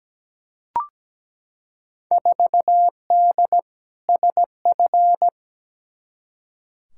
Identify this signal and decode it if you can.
Morse code, heard as a keyed tone, about 17 wpm, 700 Hz. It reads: 4D SF
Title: Field Day Exchanges Encore - 17wpm